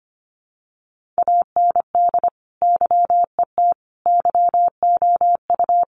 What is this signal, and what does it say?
Morse code, heard as a keyed tone, about 25 wpm, 700 Hz. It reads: AND YET YOU
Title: Sentences from Top 1400 Words 25wpm